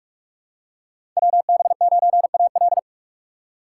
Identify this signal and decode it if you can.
Morse code, heard as a keyed tone, about 47 wpm, 700 Hz. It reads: WB9AL
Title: USA Call Signs - 5 Characters - 4x Character Spacing - 45wpm